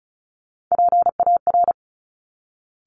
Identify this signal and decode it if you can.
Morse code, heard as a keyed tone, about 36 wpm, 700 Hz. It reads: PAR
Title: Top Three Letter Combinations in Words 35wpm